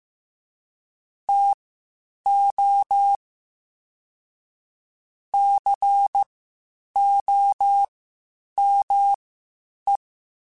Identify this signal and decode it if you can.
Morse code, heard as a keyed tone, about 15 wpm, spaced as for 8 wpm, 800 Hz. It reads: TO COME
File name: (spaced) / QSOs Spaced-mp3s / WA7QOX 1 @13 sp